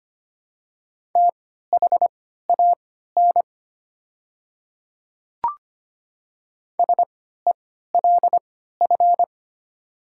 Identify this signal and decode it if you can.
Morse code, heard as a keyed tone, about 25 wpm, spaced as for 14 wpm, 700 Hz. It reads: THAN SELF
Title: 3x Character Spacing - Top 200 Words - 25wpm